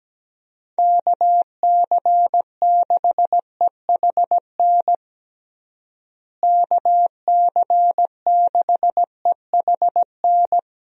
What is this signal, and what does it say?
Morse code, heard as a keyed tone, about 17 wpm, 700 Hz. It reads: KC6EHN KC6EHN
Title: USA Call Signs - 6 Characters - Repeated 3x - 17wpm